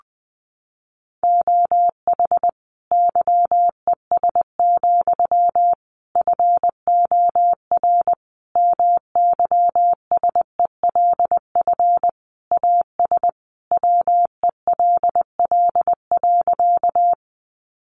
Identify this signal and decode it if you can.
Morse code, heard as a keyed tone, about 19 wpm, 700 Hz. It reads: OH YES, FOR MYSELF AS WELL.
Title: Sentences from Top 1300 Words 20wpm